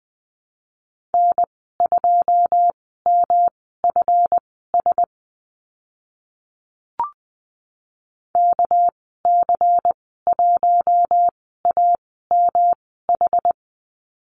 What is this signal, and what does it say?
Morse code, heard as a keyed tone, about 19 wpm, spaced as for 14 wpm, 700 Hz. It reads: N2MFS KC1AMH
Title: 231 Lesson 37g - Letter x - Call Signs 20wpm